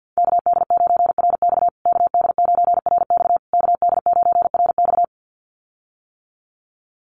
Morse code, heard as a keyed tone, about 48 wpm, 700 Hz. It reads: KD9RX KD9RX KD9RX
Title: USA Call Signs Repeated 3x 50wpm